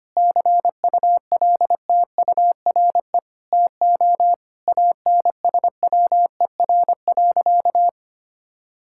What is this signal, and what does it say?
Morse code, heard as a keyed tone, about 25 wpm, 700 Hz. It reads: CULTURE TO ANSWER.
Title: Sentences from Top 1000 Words - Review Part 6 - 25wpm